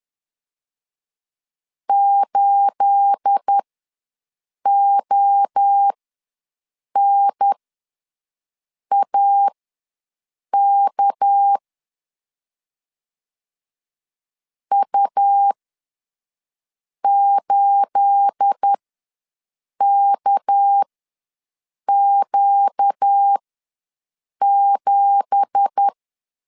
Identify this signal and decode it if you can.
Morse code, heard as a keyed tone, about 10 wpm, spaced as for 6 wpm, 800 Hz. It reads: 8ONAK U8KQ7
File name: LEZIONI / 9 2867 ws 10wpm 4215